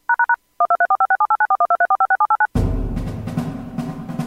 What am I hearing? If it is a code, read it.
DTMF, heard as telephone touch tones, tones of about 80 ms. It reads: #1234567894123456789